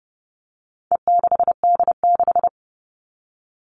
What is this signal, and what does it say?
Morse code, heard as a keyed tone, about 29 wpm, 700 Hz. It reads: E6D6
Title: All Cut Numbers - 4-Digit 30wpm